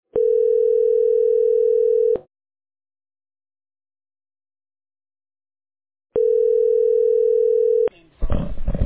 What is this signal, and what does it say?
Ringback tone (North America), 440 + 480 Hz.